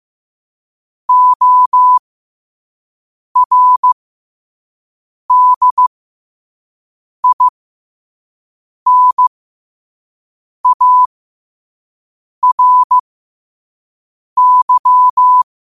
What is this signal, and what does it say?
Morse code, heard as a keyed tone, about 15 wpm, spaced as for 5 wpm, 1000 Hz. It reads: ORDINARY
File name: qotdfiles / 5wpm